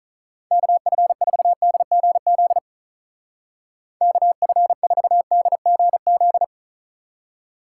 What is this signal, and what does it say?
Morse code, heard as a keyed tone, about 44 wpm, 700 Hz. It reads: KF4DGZ KF4DGZ